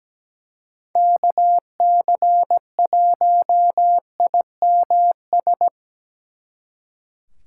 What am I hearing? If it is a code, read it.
Morse code, heard as a keyed tone, about 19 wpm, 700 Hz. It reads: KC1IMS
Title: Speed Racing - USA 2x3 Call Signs 17wpm